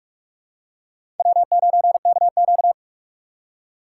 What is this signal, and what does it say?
Morse code, heard as a keyed tone, about 42 wpm, 700 Hz. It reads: W9KQ
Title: USA Call Signs - 4 Characters - 3x Character Spacing - 45wpm